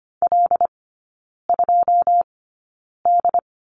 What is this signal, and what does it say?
Morse code, heard as a keyed tone, about 25 wpm, spaced as for 9 wpm, 690 Hz. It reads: L2D
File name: Lesson 14e - Number 2 - Three Characters 25wpm